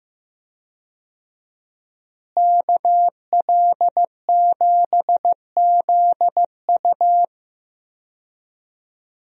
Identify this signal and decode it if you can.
Morse code, heard as a keyed tone, about 15 wpm, 700 Hz. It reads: KL7ZU
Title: USA Call Signs - 5 Characters - Repeated 2x - 15wpm